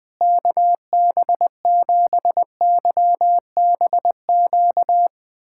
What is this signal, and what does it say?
Morse code, heard as a keyed tone, about 21 wpm, 700 Hz. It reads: KB7YBQ
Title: USA Callsign DE Callsign - 6 Characters - 20wpm